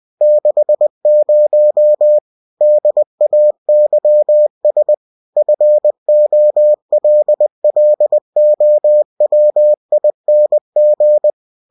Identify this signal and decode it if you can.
Morse code, heard as a keyed tone, about 20 wpm, 600 Hz. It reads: 60 DAYS FOLLOWING
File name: cw-books / winnie-the-pooh / winnie-20wpm-page-0117